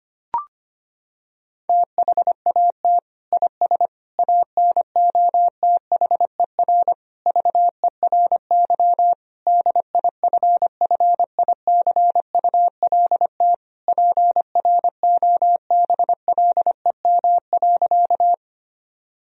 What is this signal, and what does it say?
Morse code, heard as a keyed tone, about 25 wpm, 700 Hz. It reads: THAT IS ANOTHER VERY DIFFICULT PROBLEM.
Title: Sentences from Top 1000 Words - Review Part 2 - 25wpm